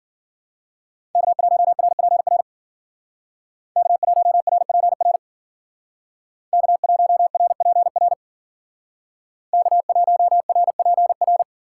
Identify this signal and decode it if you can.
Morse code, heard as a keyed tone, about 52 wpm, 700 Hz. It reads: K1RPR K1RPR K1RPR K1RPR